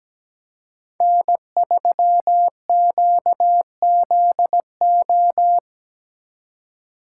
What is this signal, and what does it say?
Morse code, heard as a keyed tone, about 20 wpm, 700 Hz. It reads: N3QZO
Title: Speed Racing - USA 1x3 Call Signs 17wpm